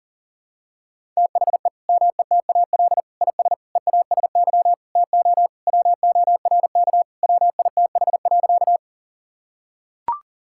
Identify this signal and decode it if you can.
Morse code, heard as a keyed tone, about 40 wpm, 700 Hz. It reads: THE METAL IS EASY TO WORK WITH.
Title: Sentences from Top 1000 Words - Review Part 6 - 40wpm